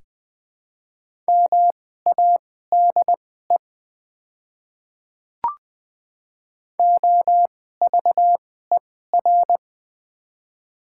Morse code, heard as a keyed tone, about 20 wpm, spaced as for 14 wpm, 700 Hz. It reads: MADE OVER
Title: 2x Character Spacing - Top 200 Words - 20wpm